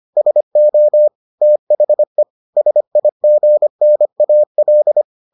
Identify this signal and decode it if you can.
Morse code, heard as a keyed tone, about 25 wpm, 600 Hz. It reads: SO THE SIGNAL